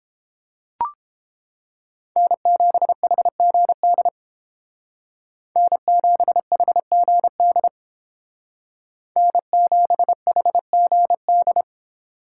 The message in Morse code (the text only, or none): N7HGD N7HGD N7HGD